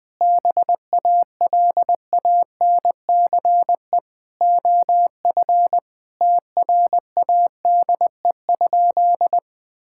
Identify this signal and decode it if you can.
Morse code, heard as a keyed tone, about 20 wpm, 700 Hz. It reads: BALANCE OF TRADE?
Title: Sentences from Top 1000 Words - Review Part 6 - 20wpm